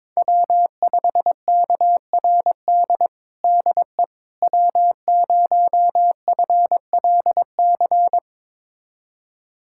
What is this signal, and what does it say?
Morse code, heard as a keyed tone, about 21 wpm, 700 Hz. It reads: W5KRD DE W0FLC
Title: USA Callsign DE Callsign - 5 Characters - 22wpm